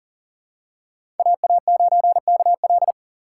Morse code, heard as a keyed tone, about 40 wpm, 700 Hz. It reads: AA9KL
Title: USA Call Signs - 5 Characters - 4x Character Spacing - 40wpm